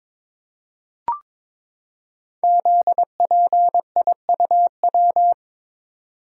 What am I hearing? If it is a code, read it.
Morse code, heard as a keyed tone, about 22 wpm, 700 Hz. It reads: ZPIUW
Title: Code Groups - 5 Letters - 22wpm